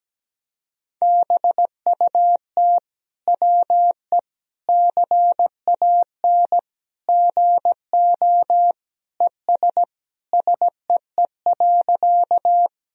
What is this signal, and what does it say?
Morse code, heard as a keyed tone, about 17 wpm, 700 Hz. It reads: BUT WE CAN GO ES SEE.
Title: Sentences from Top 100 Words and Abbreviations 17wpm